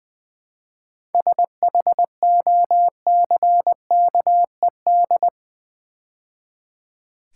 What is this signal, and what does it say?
Morse code, heard as a keyed tone, about 20 wpm, 700 Hz. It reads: SHOCKED
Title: 7 Letter Words - First 3 Letters Given - 20wpm